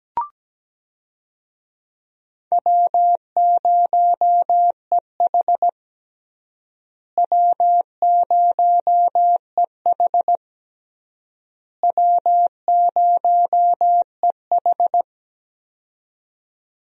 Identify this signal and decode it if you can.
Morse code, heard as a keyed tone, about 17 wpm, 700 Hz. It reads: W0EH W0EH W0EH
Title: USA Call Signs - 4 Characters - Repeated 3x - 17wpm